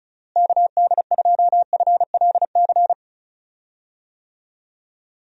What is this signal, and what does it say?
Morse code, heard as a keyed tone, about 36 wpm, 700 Hz. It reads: KD2FLC
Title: USA Call Signs - 6 Characters - Repeated 2x - 35wpm